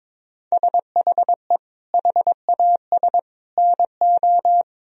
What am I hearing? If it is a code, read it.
Morse code, heard as a keyed tone, about 21 wpm, 700 Hz. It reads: SHE HAS NO
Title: Sentences from Top 1000 Words - Review Part 4 - 22wpm